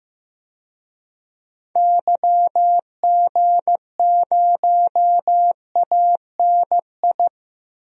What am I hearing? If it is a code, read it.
Morse code, heard as a keyed tone, about 15 wpm, 700 Hz. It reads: YG0ANI
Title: DXCC-100 Call Signs - 6 Characters - Repeated 3x - 15wpm